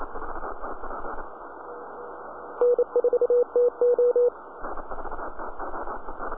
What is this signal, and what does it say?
Morse code, heard as a keyed tone, about 28 wpm, 490 Hz. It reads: N4TO